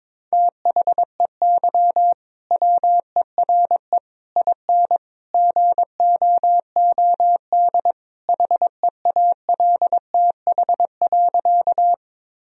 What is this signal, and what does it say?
Morse code, heard as a keyed tone, about 22 wpm, 700 Hz. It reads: THEY WERE IN GOOD HEALTH.
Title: Sentences from Top 1000 Words - Review Part 3 - 22wpm